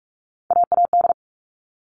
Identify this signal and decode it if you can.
Morse code, heard as a keyed tone, about 44 wpm, 700 Hz. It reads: AAD
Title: All Cut Numbers - 3-Digit 45wpm